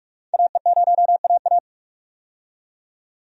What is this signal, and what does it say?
Morse code, heard as a keyed tone, about 42 wpm, 700 Hz. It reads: AE0AA
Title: USA Call Signs - 5 Characters - 45wpm